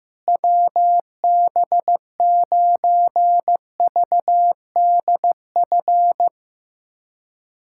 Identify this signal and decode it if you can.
Morse code, heard as a keyed tone, about 15 wpm, 700 Hz. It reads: WB9VDF
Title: USA Callsign DE Callsign - 6 Characters - 15wpm